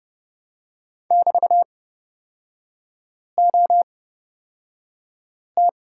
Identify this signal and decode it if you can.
Morse code, heard as a keyed tone, about 29 wpm, spaced as for 5 wpm, 700 Hz. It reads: <BT>OT
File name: Lesson 37e - Letter x - Three Characters 30wpm